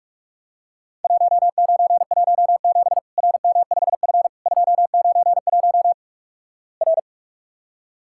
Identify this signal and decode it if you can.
Morse code, heard as a keyed tone, about 46 wpm, 700 Hz. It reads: 1917 RM5F 291 R
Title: Worked All Europe DX Contest - QTC Exchange - 45wpm